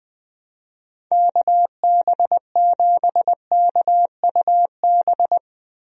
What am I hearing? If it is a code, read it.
Morse code, heard as a keyed tone, about 21 wpm, 700 Hz. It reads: KB7KUB
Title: USA Call Signs - 6 Characters - 20wpm